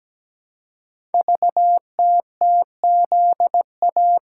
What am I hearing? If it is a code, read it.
Morse code, heard as a keyed tone, about 17 wpm, 700 Hz. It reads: VTTZA
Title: Code Groups - 5 Letters - 17wpm